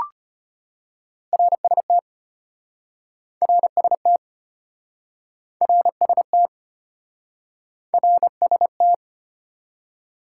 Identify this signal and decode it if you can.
Morse code, heard as a keyed tone, about 33 wpm, 700 Hz. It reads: RST RST RST RST